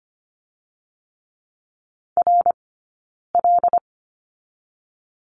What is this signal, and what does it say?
Morse code, heard as a keyed tone, about 24 wpm, 700 Hz. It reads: R L
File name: Lesson 15d - Number 5 - Two Characters 25wpm